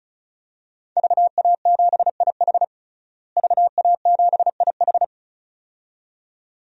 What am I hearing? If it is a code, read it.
Morse code, heard as a keyed tone, about 36 wpm, 700 Hz. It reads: VA7IH VA7IH
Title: DXCC-100 Call Signs - 5 Characters - Repeated 2x - 35wpm